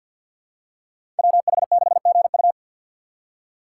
Morse code, heard as a keyed tone, about 47 wpm, 700 Hz. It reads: WH6GU
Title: USA Call Signs - 5 Characters - 3x Character Spacing - 50wpm